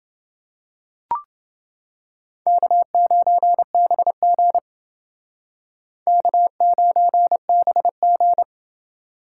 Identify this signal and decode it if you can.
Morse code, heard as a keyed tone, about 29 wpm, 700 Hz. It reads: K9BG K9BG